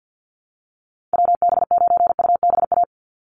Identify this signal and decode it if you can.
Morse code, heard as a keyed tone, about 50 wpm, 700 Hz. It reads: WB9UBA
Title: USA Call Signs 50wpm